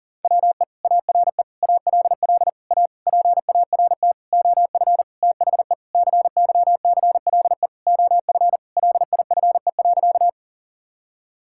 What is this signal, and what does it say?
Morse code, heard as a keyed tone, about 40 wpm, 690 Hz. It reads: WE ARE ALL A PART OF THE CYCLE OF LIFE.
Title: Sentences from Top 1000 Words - Review Part 6 - 40wpm